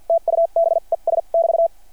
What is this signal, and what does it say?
Morse code, heard as a keyed tone, about 53 wpm, spaced as for 41 wpm, 660 Hz. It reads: TUBES <BT>
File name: archived / CRSnet